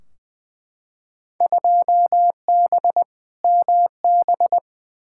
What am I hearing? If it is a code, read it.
Morse code, heard as a keyed tone, about 20 wpm, 700 Hz. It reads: 2B MB
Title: Field Day Exchanges Encore - 20wpm